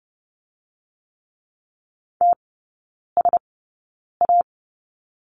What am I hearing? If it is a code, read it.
Morse code, heard as a keyed tone, about 30 wpm, spaced as for 9 wpm, 690 Hz. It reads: TSA
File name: Lesson 17e - Letter u - Three Characters 30wpm